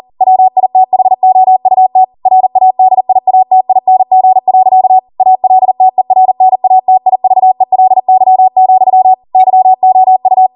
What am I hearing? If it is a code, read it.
Morse code, heard as a keyed tone, about 40 wpm, 750 Hz. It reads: WITHOUT RADIATING. ALTERNATIVELY, YOU